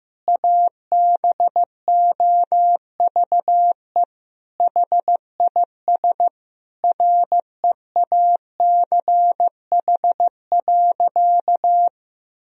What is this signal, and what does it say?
Morse code, heard as a keyed tone, about 15 wpm, 700 Hz. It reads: ABOVE HIS REACH.
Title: Sentences from Top 1000 Words - Review Part 2 - 15wpm